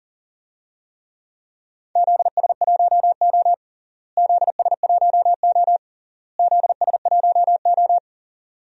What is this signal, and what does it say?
Morse code, heard as a keyed tone, about 37 wpm, 700 Hz. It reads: ZS1O ZS1O ZS1O
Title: DXCC-100 Call Signs - 4 Characters - Repeated 3x - 40wpm